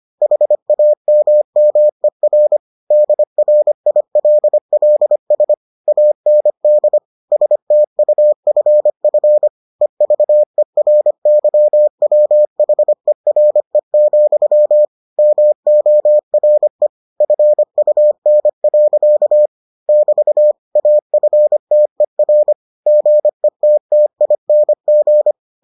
Morse code, heard as a keyed tone, about 25 wpm, 600 Hz. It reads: HAMMER DRILLS AND STUFF EVERYWHERE, MORE FUN. <BT> AFTER GETTING